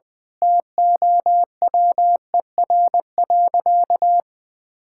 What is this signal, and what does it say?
Morse code, heard as a keyed tone, about 20 wpm, 700 Hz. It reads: TOWER.
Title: Pangram Challenge v2 - 20wpm